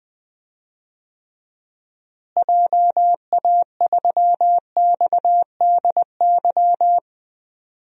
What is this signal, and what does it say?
Morse code, heard as a keyed tone, about 20 wpm, 700 Hz. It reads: JA3XDY
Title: DXCC-100 Call Signs - 6 Characters - Repeated 2x - 20wpm